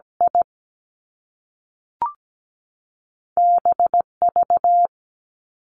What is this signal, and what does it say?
Morse code, heard as a keyed tone, about 17 wpm, 700 Hz. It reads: I BV
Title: Two Challenging Letter-Number Combinations 17wpm